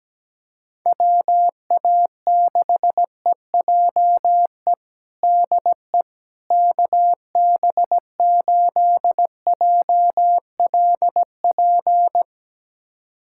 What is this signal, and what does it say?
Morse code, heard as a keyed tone, about 17 wpm, 700 Hz. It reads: WA6EJE DE KB8JLP
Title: USA Callsign DE Callsign - 6 Characters - 17wpm